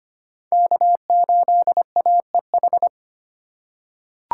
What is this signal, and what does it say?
Morse code, heard as a keyed tone, about 25 wpm, 700 Hz. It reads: K8AEH
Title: USA Call Signs - 5 Characters - 3x Character Spacing - 25wpm